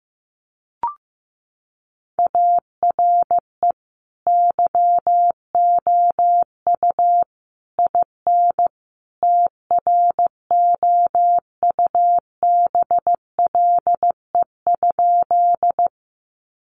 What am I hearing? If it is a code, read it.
Morse code, heard as a keyed tone, about 15 wpm, 700 Hz. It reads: ARE YOU IN TROUBLE?
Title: Sentences from Top 1200 Words 15wpm